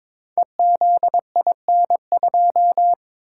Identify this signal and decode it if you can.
Morse code, heard as a keyed tone, about 22 wpm, 700 Hz. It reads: EZIN2
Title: Code Groups - 5 Letter:Numbers - 22wpm